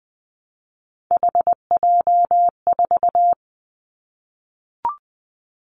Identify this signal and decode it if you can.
Morse code, heard as a keyed tone, about 20 wpm, 700 Hz. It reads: HJ4
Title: Callsign Prefixes for Top 100 DXCC - 20wpm